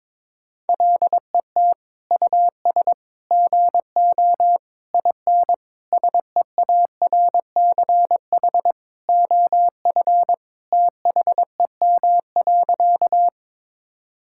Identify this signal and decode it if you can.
Morse code, heard as a keyed tone, about 22 wpm, 700 Hz. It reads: LET US GO IN SEARCH OF THEM.
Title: Sentences from Top 1000 Words - Review Part 5 - 22wpm